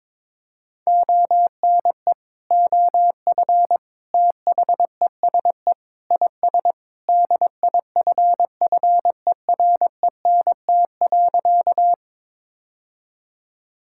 Morse code, heard as a keyed tone, about 22 wpm, 700 Hz. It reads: ONE OF THESE IS DIFFERENT.
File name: practice / 22wpm / Sentences with 2-Word Phrases - 22wpm